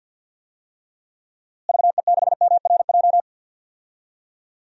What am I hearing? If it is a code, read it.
Morse code, heard as a keyed tone, about 48 wpm, 700 Hz. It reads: VE6MRJ
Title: DXCC-100 Call Signs - 6 Characters - Repeated 2x - 50wpm